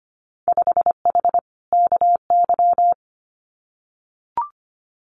Morse code, heard as a keyed tone, about 25 wpm, 700 Hz. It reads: 5H KY
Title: Winter Field Day Exchanges Encore 25wpm